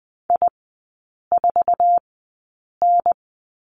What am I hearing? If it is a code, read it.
Morse code, heard as a keyed tone, about 20 wpm, spaced as for 8 wpm, 700 Hz. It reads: I4N
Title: 047 Lesson 9e - Number 4 - Three Characters 20wpm